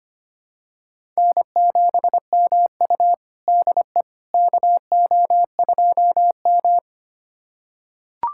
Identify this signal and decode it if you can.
Morse code, heard as a keyed tone, about 25 wpm, 700 Hz. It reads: N7MU DE KO2M
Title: USA Callsign DE Callsign - 4 Characters - 25wpm